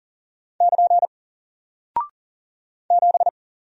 Morse code, heard as a keyed tone, about 40 wpm, 700 Hz. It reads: <KN> 7
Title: Single Character (Letter:Number:Punctuation:Prosign) 40wpm